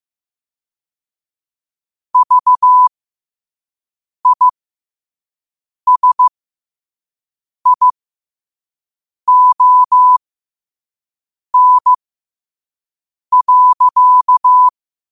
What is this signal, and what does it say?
Morse code, heard as a keyed tone, about 15 wpm, spaced as for 5 wpm, 1000 Hz. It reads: VISION.
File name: qotdfiles / 5wpm